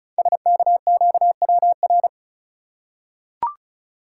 Morse code, heard as a keyed tone, about 36 wpm, 690 Hz. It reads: SKQWR
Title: Code Groups - 5 Letter:Numbers - 35wpm